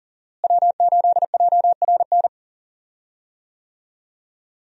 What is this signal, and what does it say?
Morse code, heard as a keyed tone, about 38 wpm, 700 Hz. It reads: W8JRN
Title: USA Call Signs - 5 Characters - Repeated 3x - 40wpm